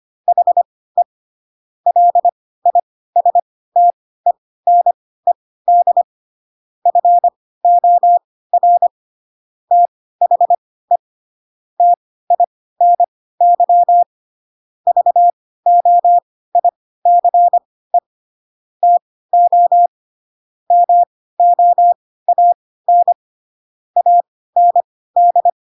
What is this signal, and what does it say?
Morse code, heard as a keyed tone, about 25 wpm, spaced as for 16 wpm, 700 Hz. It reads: HE LISTENED FOR THE TINY VOICE TO MOAN AND
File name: Pinocchio / Pinocchio ch1 10 25 ik1hge